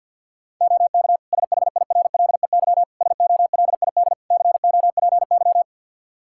Morse code, heard as a keyed tone, about 46 wpm, 700 Hz. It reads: OK SHIRLEY SOLID COPY
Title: Practice QSO - Exchange 3 - More Info - 50wpm